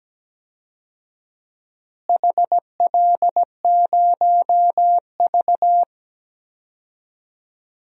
Morse code, heard as a keyed tone, about 17 wpm, 700 Hz. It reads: HL0V
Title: DXCC-100 Call Signs - 4 Characters - Repeated 3x - 17wpm